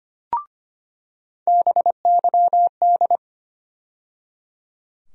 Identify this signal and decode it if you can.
Morse code, heard as a keyed tone, about 25 wpm, 700 Hz. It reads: BYD
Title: USA Call Sign Postfixes - 3 Letters 25wpm